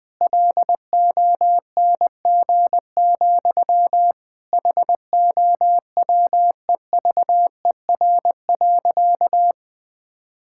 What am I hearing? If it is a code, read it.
Morse code, heard as a keyed tone, about 20 wpm, 700 Hz. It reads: LONG, HOWEVER.
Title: Sentences from Top 1000 Words - Review Part 6 - 20wpm